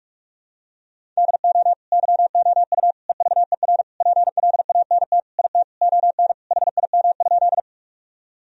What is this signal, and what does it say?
Morse code, heard as a keyed tone, about 42 wpm, 700 Hz. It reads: DO YOU EVER PLANT IT ON HIM?
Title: Sentences from Top 1000 Words - Review Part 1 - 45wpm